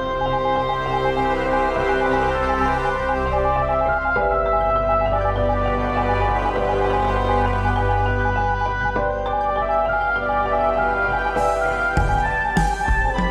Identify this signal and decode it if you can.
Morse code, heard as a keyed tone, about 20 wpm, 820 Hz. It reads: FUTURE.THE FUTURE YOU ARE